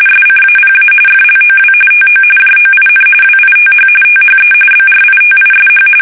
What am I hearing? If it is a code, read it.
RTTY, about 45.45 baud. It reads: SCAPA FLOW TO JOIN WITH THE BRITISH
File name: fsk850